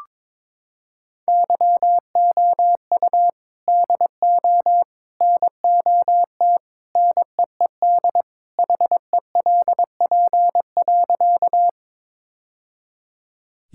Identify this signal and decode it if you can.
Morse code, heard as a keyed tone, about 22 wpm, 700 Hz. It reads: YOU DO NOT NEED HELP.
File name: practice / 22wpm / Sentences with 3-Word Phrases - 22wpm